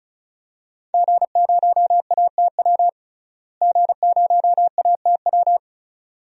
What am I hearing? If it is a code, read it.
Morse code, heard as a keyed tone, about 36 wpm, 700 Hz. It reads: G0ATW G0ATW
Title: DXCC-100 Call Signs - 5 Characters - Repeated 2x - 35wpm